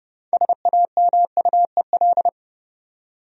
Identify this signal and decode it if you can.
Morse code, heard as a keyed tone, about 31 wpm, 700 Hz. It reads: SAMUEL
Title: USA Common Names - 30wpm